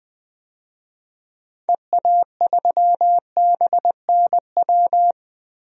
Morse code, heard as a keyed tone, about 20 wpm, 700 Hz. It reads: EA3BNW
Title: DXCC-100 Call Signs - 6 Characters - 20wpm